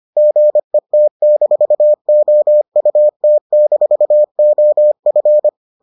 Morse code, heard as a keyed tone, about 25 wpm, 600 Hz. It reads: GET-OUT-OF